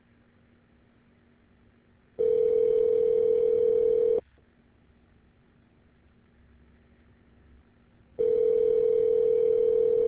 Ringback tone (North America), 440 + 480 Hz.